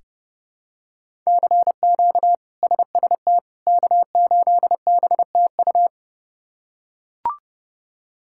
Morse code, heard as a keyed tone, about 30 wpm, 700 Hz. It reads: CQ SST K8BTU
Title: K1USN SST QSO Practice - 30wpm